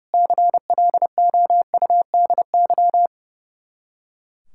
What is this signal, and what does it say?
Morse code, heard as a keyed tone, about 38 wpm, 700 Hz. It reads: CLOUDY